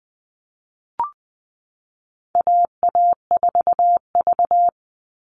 Morse code, heard as a keyed tone, about 20 wpm, 700 Hz. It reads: AA4V
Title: Call Signs Repeated 3x - 20wpm